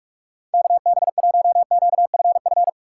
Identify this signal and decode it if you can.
Morse code, heard as a keyed tone, about 46 wpm, 700 Hz. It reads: KB1QFF
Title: USA Call Signs - 6 Characters - 6x Character Spacing - 45wpm